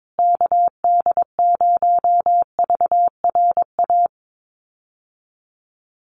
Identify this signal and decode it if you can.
Morse code, heard as a keyed tone, about 22 wpm, 700 Hz. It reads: KD0VRA
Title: USA Call Signs Repeated 2x 22wpm